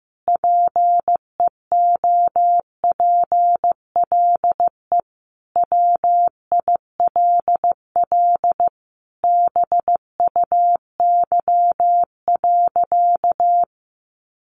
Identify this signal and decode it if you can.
Morse code, heard as a keyed tone, about 15 wpm, 700 Hz. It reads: PEOPLE WILL BUY.
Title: Sentences from Top 1300 Words 15wpm